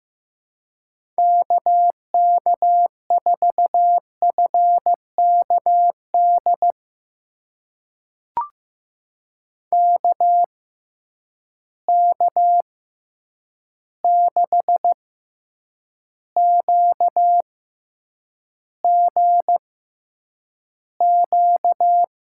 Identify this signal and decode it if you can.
Morse code, heard as a keyed tone, about 15 wpm, spaced as for 8 wpm, 700 Hz. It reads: KK4FKD KK6QGQ
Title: USA Call Signs - 6 Characters - 6x Character Spacing - 15wpm